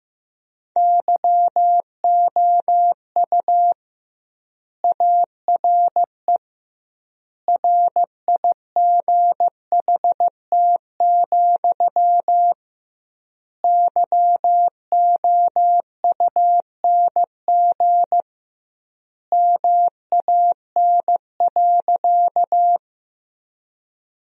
Morse code, heard as a keyed tone, about 15 wpm, 700 Hz. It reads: YOU ARE RIGHT, YOUNG MAN.
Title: 2x Word-Spacing - Sentences from Top 400 Words - 15wpm